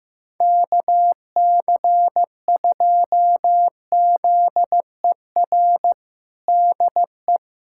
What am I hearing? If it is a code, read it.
Morse code, heard as a keyed tone, about 15 wpm, 700 Hz. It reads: KC2ZER DE
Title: USA Callsign DE Callsign 15wpm